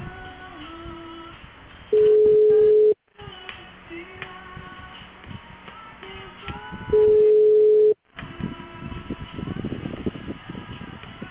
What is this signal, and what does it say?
Ringback tone (Europe, 425 Hz).